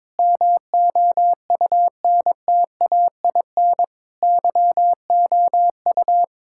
Morse code, heard as a keyed tone, about 22 wpm, 700 Hz. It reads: MOUNTAIN YOU
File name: practice / 22wpm / Sentences with 3-Word Phrases - 22wpm